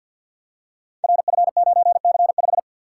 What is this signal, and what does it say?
Morse code, heard as a keyed tone, about 48 wpm, 700 Hz. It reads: RF9C5
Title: Code Groups - 5 Letter:Numbers - 50wpm